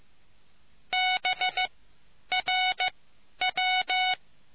Morse code, heard as a keyed tone, about 15 wpm, spaced as for 10 wpm, 740 Hz. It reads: BRW